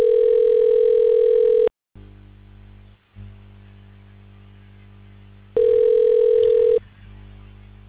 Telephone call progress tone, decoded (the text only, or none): ringback tone